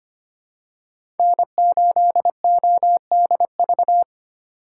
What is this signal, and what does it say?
Morse code, heard as a keyed tone, about 25 wpm, 700 Hz. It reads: N8ODV
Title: USA Call Signs - 4x Character Spacing - 25wpm